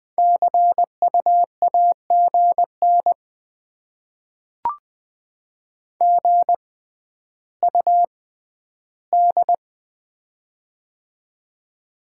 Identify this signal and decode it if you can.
Morse code, heard as a keyed tone, about 20 wpm, spaced as for 11 wpm, 700 Hz. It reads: CU AGN GUD
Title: Top 20 QSO Elements - 6x Character Spacing - 20wpm